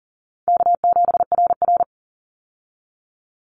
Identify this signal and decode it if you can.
Morse code, heard as a keyed tone, about 40 wpm, 700 Hz. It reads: K7RR
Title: Call Signs Repeated 2x - 40wpm